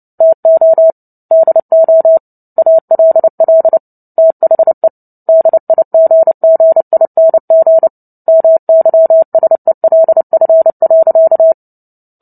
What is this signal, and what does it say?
Morse code, heard as a keyed tone, about 30 wpm, 650 Hz. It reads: TO DO ALL THE DIGGING MYSELF.
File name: Book Two / 03-Chapter 3 - 29 WPM